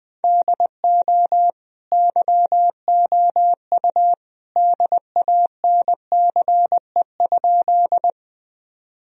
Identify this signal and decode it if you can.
Morse code, heard as a keyed tone, about 19 wpm, 700 Hz. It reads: DO YOU DANCE?
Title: Sentences from Top 1000 Words - Review Part 4 - 20wpm